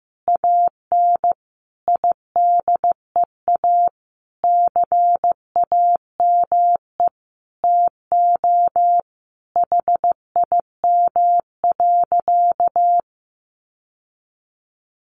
Morse code, heard as a keyed tone, about 15 wpm, 700 Hz. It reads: AN IDEA CAME TO HIM.
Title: Sentences from Top 1000 Words 15wpm